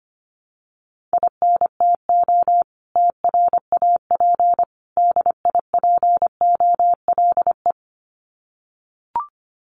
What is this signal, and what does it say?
Morse code, heard as a keyed tone, about 25 wpm, 700 Hz. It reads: INTO TRAP DIPOLE
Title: Common QSO Phrases 25wpm